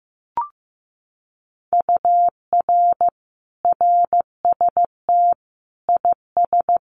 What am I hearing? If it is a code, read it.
Morse code, heard as a keyed tone, about 15 wpm, 700 Hz. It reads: UR RST IS
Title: Common QSO Phrases Encore 15wpm